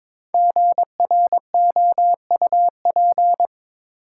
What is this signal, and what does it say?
Morse code, heard as a keyed tone, about 22 wpm, 700 Hz. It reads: GROUP
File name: 5 Letter Words - First 3 Letters Given - 22wpm